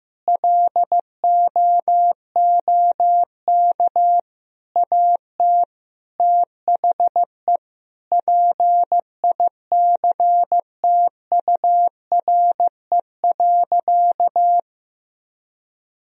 Morse code, heard as a keyed tone, about 15 wpm, 700 Hz. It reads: LOOK AT THE PICTURE.
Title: Sentences from Top 1000 Words - Review Part 1 - 15wpm